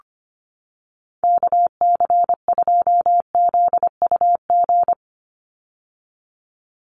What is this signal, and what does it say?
Morse code, heard as a keyed tone, about 25 wpm, 700 Hz. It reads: KC2ZUG
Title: USA Call Signs 25wpm